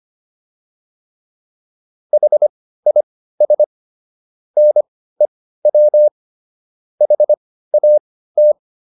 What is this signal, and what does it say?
Morse code, heard as a keyed tone, about 25 wpm, spaced as for 15 wpm, 600 Hz. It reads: HIS NEW HAT